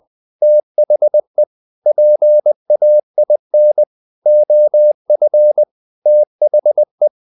Morse code, heard as a keyed tone, about 20 wpm, 600 Hz. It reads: THE PAIN OF THE
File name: the old man and the sea / the old man and the sea part 1 20wpm